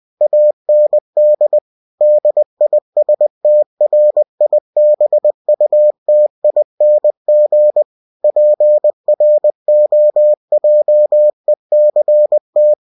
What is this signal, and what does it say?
Morse code, heard as a keyed tone, about 20 wpm, 600 Hz. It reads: AND DISTRIBUTING PROJECT
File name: cw-books / winnie-the-pooh / winnie-20wpm-page-0112